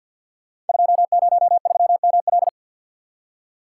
Morse code, heard as a keyed tone, about 52 wpm, 700 Hz. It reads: 203ML
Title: Code Groups - 5 Letter:Numbers - 50wpm